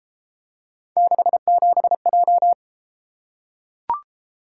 Morse code, heard as a keyed tone, about 33 wpm, 700 Hz. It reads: 67J